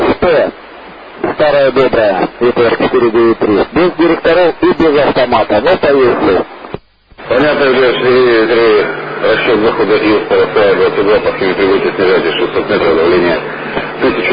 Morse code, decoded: none